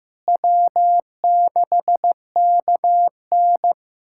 Morse code, heard as a keyed tone, about 15 wpm, 700 Hz. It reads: W6KN
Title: SOTA QSO Practice 15wpm